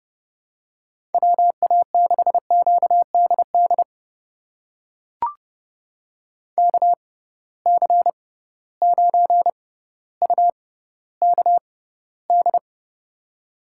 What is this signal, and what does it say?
Morse code, heard as a keyed tone, about 30 wpm, spaced as for 15 wpm, 700 Hz. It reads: WA6QDD KC9UKD
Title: USA Call Signs - 6 Characters - 6x Character Spacing - 30wpm